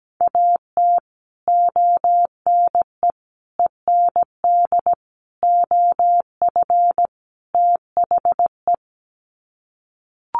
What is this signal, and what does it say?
Morse code, heard as a keyed tone, about 17 wpm, 700 Hz. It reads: AT ONE END OF THE
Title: Sets of 5 Words Encore 17wpm